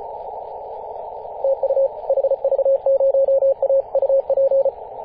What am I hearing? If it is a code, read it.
Morse code, heard as a keyed tone, about 34 wpm, 540 Hz, with radio noise. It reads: TU HV0A UP
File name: 2016/Apr/09 0838z HV0A 18.071MHz CW